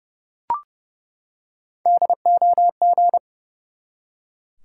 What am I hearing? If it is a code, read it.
Morse code, heard as a keyed tone, about 29 wpm, 700 Hz. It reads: DOG
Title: Three Letter Words - 30wpm